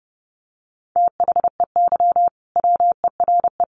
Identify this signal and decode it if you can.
Morse code, heard as a keyed tone, about 29 wpm, 700 Hz. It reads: THEY WERE
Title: Sentences from Top 800 Words 30wpm